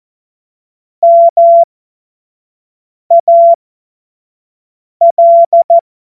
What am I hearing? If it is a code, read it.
Morse code, heard as a keyed tone, about 14 wpm, spaced as for 5 wpm, 680 Hz. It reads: MAL
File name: Witz-16-September-2025-25-F70